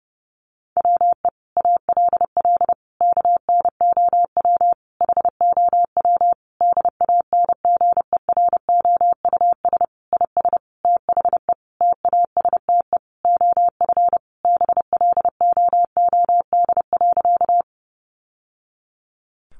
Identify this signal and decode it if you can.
Morse code, heard as a keyed tone, about 30 wpm, 700 Hz. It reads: WE ALL KNOW HOW DANGEROUS IS THE TASTE OF BLOOD.
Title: Sentences from Top 1200 Words 30wpm